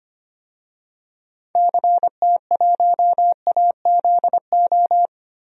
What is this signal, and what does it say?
Morse code, heard as a keyed tone, about 25 wpm, 700 Hz. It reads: CT1AZO
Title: DXCC-100 Call Signs - 6 Characters - 25wpm